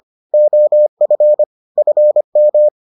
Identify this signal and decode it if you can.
Morse code, heard as a keyed tone, about 25 wpm, 600 Hz. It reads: OF FM